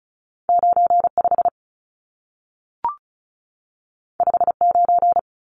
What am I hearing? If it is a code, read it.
Morse code, heard as a keyed tone, about 36 wpm, 700 Hz. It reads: 95 59
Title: Numbers - 2-Digit 35wpm